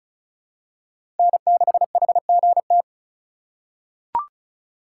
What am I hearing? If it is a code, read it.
Morse code, heard as a keyed tone, about 36 wpm, 700 Hz. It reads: N6HGT
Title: USA Call Signs - 5 Characters - 3x Character Spacing - 35wpm